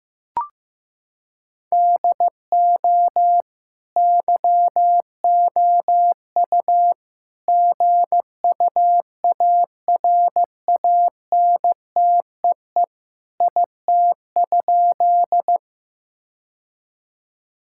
Morse code, heard as a keyed tone, about 16 wpm, 700 Hz. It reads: DO YOU GUARANTEE IT?
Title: Sentences from Top 1600 Words 15wpm